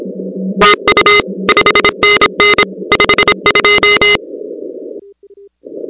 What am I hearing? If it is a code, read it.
Morse code, heard as a keyed tone, about 26 wpm, 2050 Hz, with noise. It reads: EU 5NN 52
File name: contests / 2019-cw-wpx-kt5j-k5pi- -2330 0-13